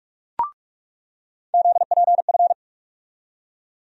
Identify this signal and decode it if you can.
Morse code, heard as a keyed tone, about 44 wpm, 700 Hz. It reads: ZPF
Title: USA Call Sign Postfixes - 3 Letters 45wpm